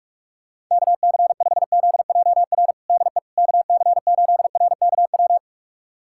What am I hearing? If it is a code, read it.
Morse code, heard as a keyed tone, about 42 wpm, 700 Hz. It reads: KC5ZJR DE KC8RKW
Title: USA Callsign DE Callsign - 6 Characters - 45wpm